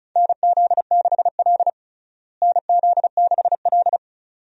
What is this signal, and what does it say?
Morse code, heard as a keyed tone, about 34 wpm, 700 Hz. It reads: NZ6L NZ6L
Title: USA Call Signs - 4 Characters - Repeated 2x - 35wpm